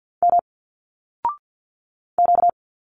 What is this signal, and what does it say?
Morse code, heard as a keyed tone, about 47 wpm, 700 Hz. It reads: M Q
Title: Single Letters 50wpm